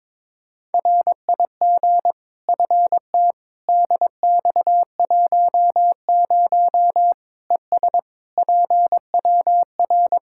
Morse code, heard as a keyed tone, about 22 wpm, 700 Hz. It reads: RIG FT DX10 ES PWR
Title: Practice QSO - Exchange 2 - Info - 22wpm